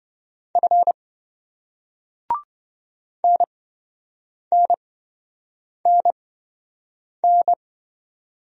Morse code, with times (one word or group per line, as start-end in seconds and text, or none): 0.6-0.9 F
3.2-7.5 NNNN